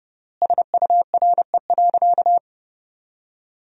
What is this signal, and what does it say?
Morse code, heard as a keyed tone, about 30 wpm, 700 Hz. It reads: SURE.
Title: 3x Word-Spacing - Sentences from Top 500 Words - 30wpm